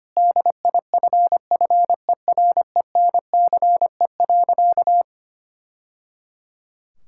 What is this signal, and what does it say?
Morse code, heard as a keyed tone, about 25 wpm, 700 Hz. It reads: DIFFERENCE.
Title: Sentences from Top 1000 Words - Review Part 5 - 25wpm